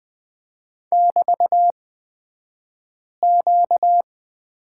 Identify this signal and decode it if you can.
Morse code, heard as a keyed tone, about 20 wpm, spaced as for 5 wpm, 700 Hz. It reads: <BT>Q
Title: 235 Lesson 38d - Letter q - Two Characters 20wpm